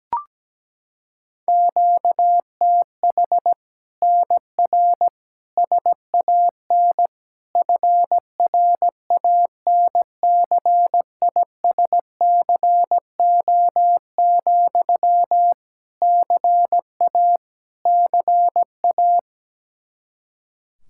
Morse code, heard as a keyed tone, about 17 wpm, 700 Hz. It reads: QTH NR SAN FRANCISCO, CA CA
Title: Practice QSO - Exchange 1 - Beginning - 17wpm